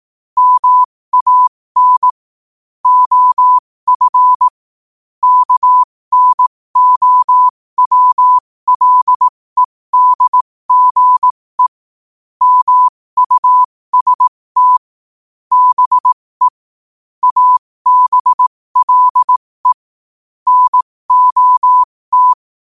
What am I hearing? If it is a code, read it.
Morse code, heard as a keyed tone, about 18 wpm, spaced as for 15 wpm, 1000 Hz. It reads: MAN OF KNOWLEDGE MUST BE ABLE NOT
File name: qotdfiles / 15wpm